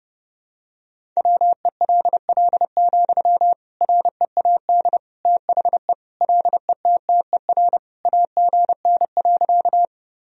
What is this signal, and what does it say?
Morse code, heard as a keyed tone, about 30 wpm, 700 Hz. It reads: WELL, READ THE LETTER AGN.
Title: Sentences from Top 300 Words and Abbreviations 30wpm